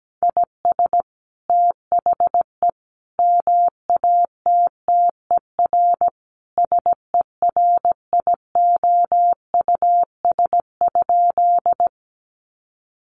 Morse code, heard as a keyed tone, about 17 wpm, 700 Hz. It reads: IS THE MATTER SERIOUS?
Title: Sentences from Top 1100 Words 17wpm